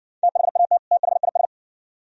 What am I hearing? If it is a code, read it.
Morse code, heard as a keyed tone, about 63 wpm, 700 Hz. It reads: THAT THIS